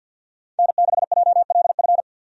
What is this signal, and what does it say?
Morse code, heard as a keyed tone, about 53 wpm, 700 Hz. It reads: N6JLF
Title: USA Call Signs - 5 Characters - 50wpm